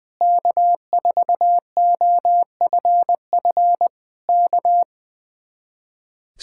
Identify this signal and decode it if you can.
Morse code, heard as a keyed tone, about 20 wpm, 700 Hz. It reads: K4OFF K
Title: Speed-Racing Common QSO Phrases - 20wpm